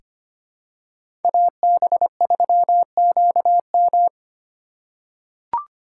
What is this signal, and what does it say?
Morse code, heard as a keyed tone, about 25 wpm, 700 Hz. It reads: AB3QM
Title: USA Call Signs - 5 Characters - 2x Character Spacing - 25wpm